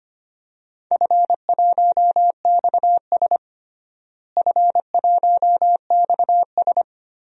Morse code, heard as a keyed tone, about 25 wpm, 700 Hz. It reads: F1XS F1XS
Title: DXCC-100 Call Signs - 4 Characters - Repeated 2x - 25wpm